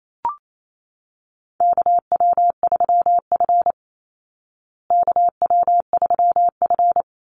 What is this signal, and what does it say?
Morse code, heard as a keyed tone, about 28 wpm, 700 Hz. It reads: KW3F KW3F
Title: Call Signs Repeated 2x - 28wpm